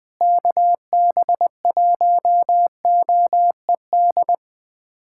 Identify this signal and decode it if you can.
Morse code, heard as a keyed tone, about 20 wpm, 700 Hz. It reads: KB1OED
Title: USA Callsign DE Callsign - 6 Characters - 20wpm